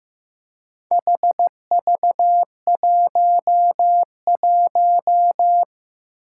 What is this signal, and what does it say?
Morse code, heard as a keyed tone, about 15 wpm, 700 Hz. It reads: HV11
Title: Four Challenging Letter-Number Combinations 15wpm